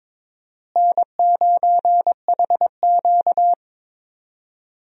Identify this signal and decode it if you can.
Morse code, heard as a keyed tone, about 22 wpm, 700 Hz. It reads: N9HQ
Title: USA Call Signs - 4 Characters - Repeated 2x - 22wpm